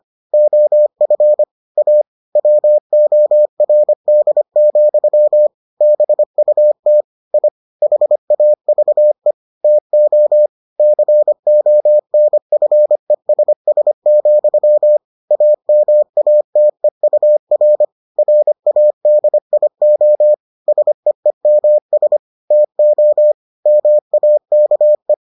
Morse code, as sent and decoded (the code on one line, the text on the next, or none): --- ..-. / .- / .-- --- .-. -.. --..-- / -... ..- - / .. / .... .- ...- . / - --- / -.-. --- -. ..-. ... ... --..-- / .- -- .- - . ..- .-. / .-. .- -.. .. --- / ... . -- ... / - --- / -- .- -.- .
OF A WORD, BUT I HAVE TO CONFESS, AMATEUR RADIO SEEMS TO MAKE